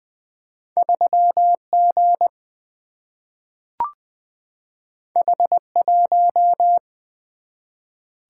Morse code, heard as a keyed tone, about 20 wpm, 700 Hz. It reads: 3G H1
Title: Single-Letter Single-Number 20wpm